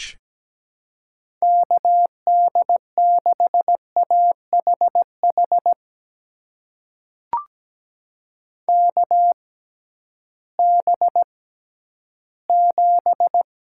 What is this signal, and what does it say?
Morse code, heard as a keyed tone, about 17 wpm, spaced as for 11 wpm, 700 Hz. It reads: KD6AHH KB7
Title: USA Call Signs - 6 Characters - 6x Character Spacing - 17wpm